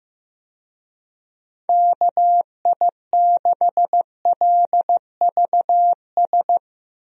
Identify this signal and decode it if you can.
Morse code, heard as a keyed tone, about 15 wpm, 700 Hz. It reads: KI6LVS
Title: USA Call Signs - 6 Characters - 15wpm